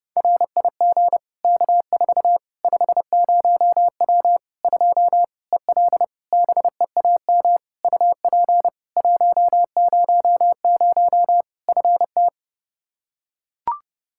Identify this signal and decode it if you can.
Morse code, heard as a keyed tone, about 32 wpm, 700 Hz. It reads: RIG K4 50W 2 EL BEAM UP 100 FT
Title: Practice QSO - Exchange 2 - Info - 30wpm